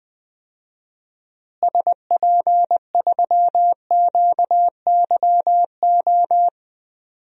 Morse code, heard as a keyed tone, about 20 wpm, 700 Hz. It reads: SP3QYO
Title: DXCC-100 Call Signs - 6 Characters - Repeated 3x - 20wpm